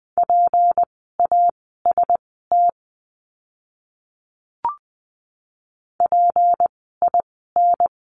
Morse code, heard as a keyed tone, about 20 wpm, spaced as for 15 wpm, 700 Hz. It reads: PAST PIN
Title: 153 Lesson 25f - Letter p - Words 20wpm